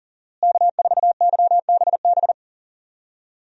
Morse code, heard as a keyed tone, about 40 wpm, 700 Hz. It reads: K4YBB
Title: USA Call Signs - 5 Characters - Repeated 2x - 40wpm